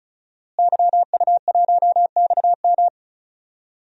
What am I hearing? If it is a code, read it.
Morse code, heard as a keyed tone, about 36 wpm, 700 Hz. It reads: YU1XM
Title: DXCC-100 Call Signs - 5 Characters - Repeated 3x - 35wpm